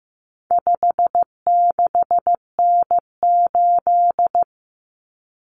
Morse code, heard as a keyed tone, about 15 wpm, 700 Hz. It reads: 56N8
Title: Common Cut Numbers - 4-Digit 15wpm